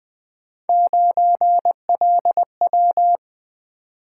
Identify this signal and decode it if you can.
Morse code, heard as a keyed tone, about 20 wpm, 700 Hz. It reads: 9LW
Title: Three Letter-Number Combinations - 20wpm